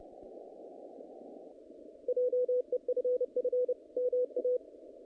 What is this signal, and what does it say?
Morse code, heard as a keyed tone, about 30 wpm, 500 Hz. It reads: JEFF MA
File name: contests / naqp-cw-k5tr-op-k5pi- -2100